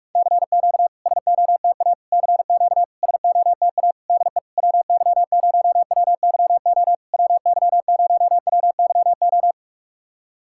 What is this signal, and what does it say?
Morse code, heard as a keyed tone, about 44 wpm, 690 Hz. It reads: CQ SOTA CQ SOTA DE WY0WYO WY0WYO
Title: SOTA QSO Practice 45wpm